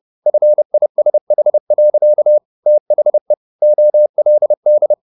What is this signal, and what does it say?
Morse code, heard as a keyed tone, about 30 wpm, 600 Hz. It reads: FISH. THE OLD
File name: the old man and the sea / the old man and the sea 1 30wpm